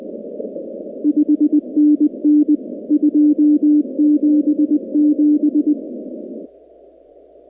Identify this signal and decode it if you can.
Morse code, heard as a keyed tone, about 20 wpm, 300 Hz. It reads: 5NN 277
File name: contests / 2019-cw-wpx-kt5j-k5pi- -2330 0-19